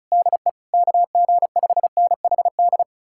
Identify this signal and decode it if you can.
Morse code, heard as a keyed tone, about 46 wpm, 700 Hz. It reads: DE KG5NHD